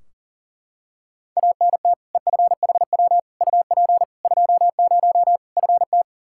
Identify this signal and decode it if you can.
Morse code, heard as a keyed tone, about 40 wpm, 700 Hz. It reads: ANT EFHW UP 20 FT
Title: Practice QSO - Exchange 2 - Info - 40wpm